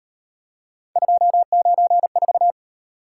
Morse code, heard as a keyed tone, about 38 wpm, 700 Hz. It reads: 294